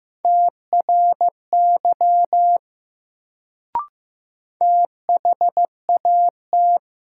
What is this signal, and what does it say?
Morse code, heard as a keyed tone, about 16 wpm, 700 Hz. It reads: TRY THAT
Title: Sets of 2 Words - New Word Drill - 15wpm